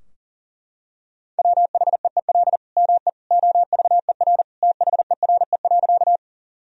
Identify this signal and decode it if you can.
Morse code, heard as a keyed tone, about 37 wpm, 700 Hz. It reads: WHEEL ME OVER THERE.
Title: Sentences from Top 1000 Words - Review Part 3 - 40wpm